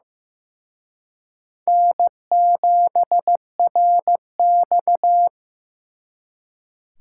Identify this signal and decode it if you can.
Morse code, heard as a keyed tone, about 18 wpm, 700 Hz. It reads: N7RX
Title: Speed Racing - USA 1x2 Call Signs 15wpm